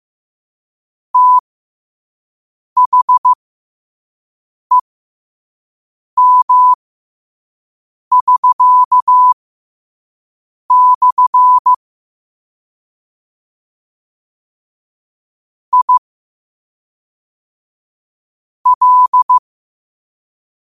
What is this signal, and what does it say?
Morse code, heard as a keyed tone, about 15 wpm, spaced as for 5 wpm, 1000 Hz. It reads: THEM<SK>/ I L
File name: qotdfiles / 5wpm